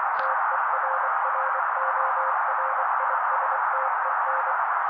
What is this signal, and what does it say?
Morse code, heard as a keyed tone, about 24 wpm, 530 Hz. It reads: TERRORISTEN
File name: daily / 23wpm noise